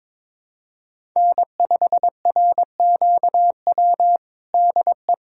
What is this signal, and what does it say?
Morse code, heard as a keyed tone, about 21 wpm, 700 Hz. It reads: N5RQW DE
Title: USA Callsign DE Callsign 22wpm